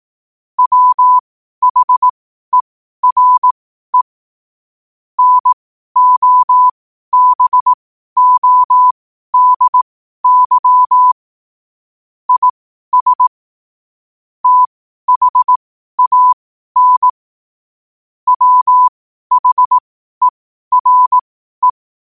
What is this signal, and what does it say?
Morse code, heard as a keyed tone, about 18 wpm, spaced as for 12 wpm, 1000 Hz. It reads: WHERE NOBODY IS THAN WHERE
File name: qotdfiles / 12wpm